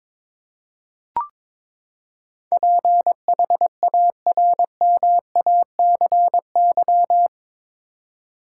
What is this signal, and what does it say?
Morse code, heard as a keyed tone, about 22 wpm, 700 Hz. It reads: PHARMACY
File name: practice / 22wpm / Eight Letter Words - 22wpm